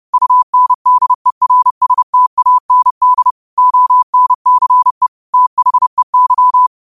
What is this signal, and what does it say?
Morse code, heard as a keyed tone, about 30 wpm, 1000 Hz. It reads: UNDERSTAND ONCE THEY
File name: qotdfiles / 30wpm